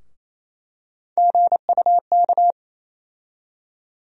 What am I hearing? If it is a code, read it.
Morse code, heard as a keyed tone, about 28 wpm, 700 Hz. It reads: GUK
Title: USA Call Sign Postfixes - 3 Letters 28wpm